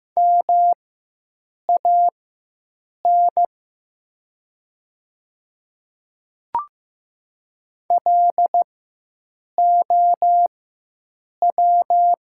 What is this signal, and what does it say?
Morse code, heard as a keyed tone, about 15 wpm, spaced as for 7 wpm, 700 Hz. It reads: MAN LOW
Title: 4x Character Spacing - Top 200 Words - 15wpm